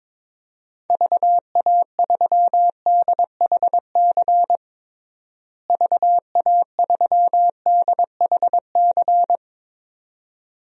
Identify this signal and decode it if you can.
Morse code, heard as a keyed tone, about 21 wpm, 700 Hz. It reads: VA3DHC VA3DHC
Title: DXCC-100 Call Signs - 6 Characters - Repeated 2x - 22wpm